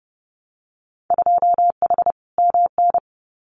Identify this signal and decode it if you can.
Morse code, heard as a keyed tone, about 30 wpm, 700 Hz. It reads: 2H MN
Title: Winter Field Day Exchanges 30wpm